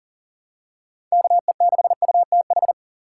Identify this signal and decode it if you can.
Morse code, heard as a keyed tone, about 40 wpm, 700 Hz. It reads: KE6UTH
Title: USA Call Signs - 6 Characters - 6x Character Spacing - 40wpm